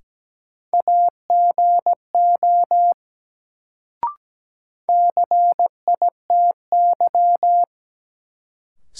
Morse code, heard as a keyed tone, about 17 wpm, 700 Hz. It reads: AGO CITY
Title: Sets of 2 Words - New Word Drill - 17wpm